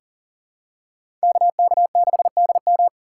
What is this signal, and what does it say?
Morse code, heard as a keyed tone, about 40 wpm, 700 Hz. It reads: KK6DM
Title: USA Call Signs - 4x Character Spacing - 40wpm